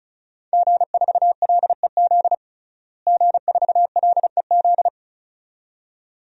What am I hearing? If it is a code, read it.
Morse code, heard as a keyed tone, about 36 wpm, 700 Hz. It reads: G4LEZ G4LEZ
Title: DXCC-100 Call Signs - 5 Characters - Repeated 2x - 35wpm